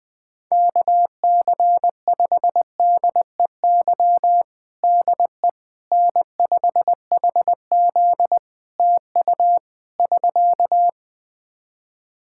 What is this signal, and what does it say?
Morse code, heard as a keyed tone, about 20 wpm, 700 Hz. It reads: KC5DEY DE N5HZ TU <SK>
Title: Practice QSO - Exchange 4 - Ending - 20wpm